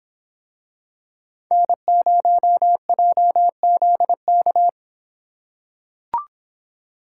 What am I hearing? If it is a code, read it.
Morse code, heard as a keyed tone, about 24 wpm, 700 Hz. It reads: N0JZK